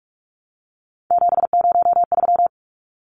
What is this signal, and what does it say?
Morse code, heard as a keyed tone, about 44 wpm, 700 Hz. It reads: 703
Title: Numbers - 3-Digit 45wpm